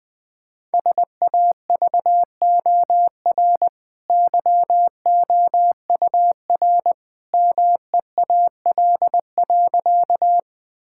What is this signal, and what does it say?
Morse code, heard as a keyed tone, about 20 wpm, 700 Hz. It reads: SAVOR YOUR MEAL.
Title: Easy Sentences with Hints - 20wpm